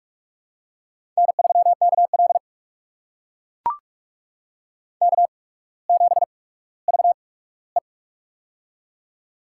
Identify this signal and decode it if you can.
Morse code, heard as a keyed tone, about 44 wpm, spaced as for 19 wpm, 700 Hz. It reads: N3KL K7VE
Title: USA Call Signs - 4 Characters - 8x Character Spacing - 45wpm